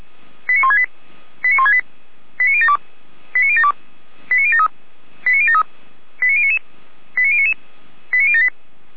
Five-tone selective call (ZVEI), heard as quick runs of five tones, tones of about 70 ms. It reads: 89168 89168 89072 89072 89073 89073 89099 89099 89078